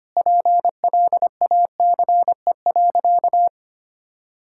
Morse code, heard as a keyed tone, about 25 wpm, 700 Hz. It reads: PLACE.
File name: practice / 25wpm / Sentences with 2-Word Phrases - 25wpm